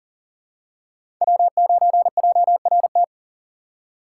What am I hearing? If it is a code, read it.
Morse code, heard as a keyed tone, about 38 wpm, 700 Hz. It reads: W9JRT
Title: USA Call Signs - 5 Characters - 6x Character Spacing - 40wpm